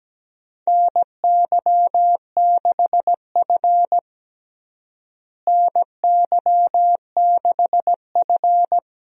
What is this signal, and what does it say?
Morse code, heard as a keyed tone, about 17 wpm, 700 Hz. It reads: NY6F NY6F
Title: USA Call Signs - 4 Characters - Repeated 3x - 17wpm